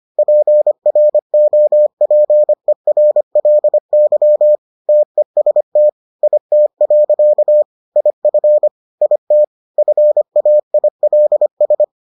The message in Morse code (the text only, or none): PROPERLY TEST IT. IF IT FAILS